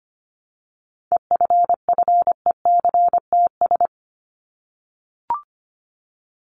Morse code, heard as a keyed tone, about 25 wpm, 700 Hz. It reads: EFFECTS
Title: ES:S Suffix - Top 100 Words with Base Words 25wpm